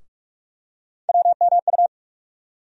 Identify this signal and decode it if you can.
Morse code, heard as a keyed tone, about 46 wpm, 700 Hz. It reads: WMU
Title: USA Call Sign Postfixes - 3 Letters 45wpm